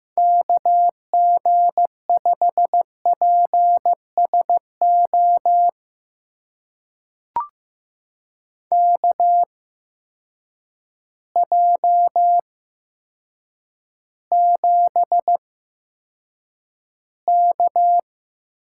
Morse code, heard as a keyed tone, about 15 wpm, spaced as for 8 wpm, 700 Hz. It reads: KG5PSO KJ7K
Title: USA Call Signs - 8x Character Spacing - 15wpm